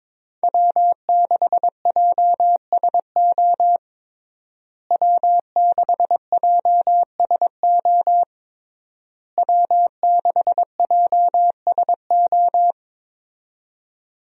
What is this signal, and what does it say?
Morse code, heard as a keyed tone, about 22 wpm, 700 Hz. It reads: W6JSO W6JSO W6JSO
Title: USA Call Signs - 5 Characters - Repeated 3x - 22wpm